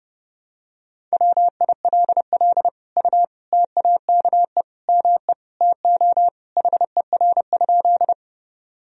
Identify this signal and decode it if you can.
Morse code, heard as a keyed tone, about 30 wpm, 700 Hz. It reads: WILL U TAKE ME TO HER?
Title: Sentences from Top 200 Words and Abbreviations 30wpm